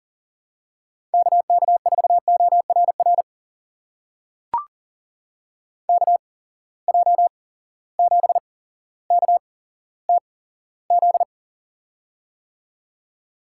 Morse code, heard as a keyed tone, about 40 wpm, spaced as for 17 wpm, 700 Hz. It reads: KK4ORR KJ7KTZ
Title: USA Call Signs - 6 Characters - 8x Character Spacing - 40wpm